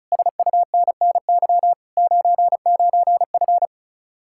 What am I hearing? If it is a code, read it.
Morse code, heard as a keyed tone, about 42 wpm, 700 Hz. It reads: SUNNY 99F
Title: Speed-Racing Common QSO Phrases Encore - 35wpm